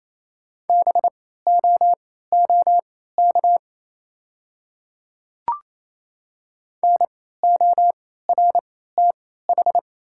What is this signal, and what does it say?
Morse code, heard as a keyed tone, about 27 wpm, spaced as for 16 wpm, 700 Hz. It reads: BOOK NORTH
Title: 3x Character Spacing - Top 500 Words - 28wpm